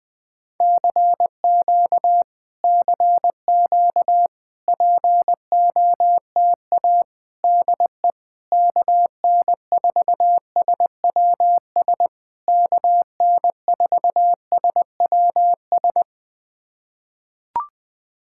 Morse code, heard as a keyed tone, about 20 wpm, 700 Hz. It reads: CQ CQ POTA DE KN4SWS KN4SWS
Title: POTA QSO Practice - 20wpm